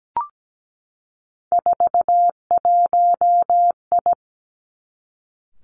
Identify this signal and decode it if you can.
Morse code, heard as a keyed tone, about 17 wpm, 700 Hz. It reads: 41I
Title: Three Challenging Letter-Number Combinations 17wpm